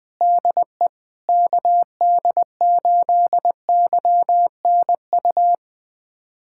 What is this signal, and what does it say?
Morse code, heard as a keyed tone, about 20 wpm, 700 Hz. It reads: DE KD8YNU
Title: USA Callsign DE Callsign - 6 Characters - 20wpm